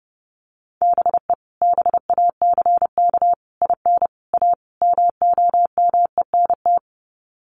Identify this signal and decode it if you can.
Morse code, heard as a keyed tone, about 30 wpm, 700 Hz. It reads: BE BACK IN A MOMENT
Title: Sets of 5 Words 30wpm